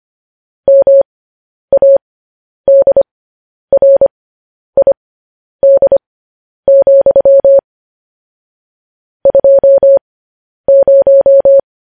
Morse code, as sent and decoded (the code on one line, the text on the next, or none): -- .- -.. .-. .. -.. --..-- / ..--- -----
MADRID, 20